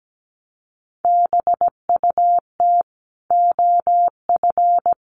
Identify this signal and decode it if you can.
Morse code, heard as a keyed tone, about 17 wpm, 700 Hz. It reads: BUT OF
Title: Sentences from Top 1500 Words 17wpm